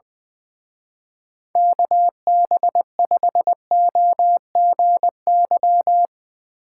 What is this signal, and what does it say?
Morse code, heard as a keyed tone, about 22 wpm, 700 Hz. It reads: KB5OGY
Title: Speed Racing - USA Call Signs 20wpm